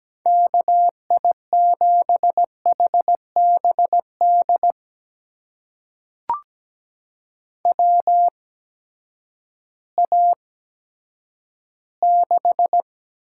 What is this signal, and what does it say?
Morse code, heard as a keyed tone, about 17 wpm, spaced as for 10 wpm, 700 Hz. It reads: KI7HBD WA6
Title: USA Call Signs - 8x Character Spacing - 17wpm